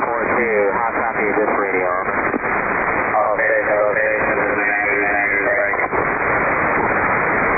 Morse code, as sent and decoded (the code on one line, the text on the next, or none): none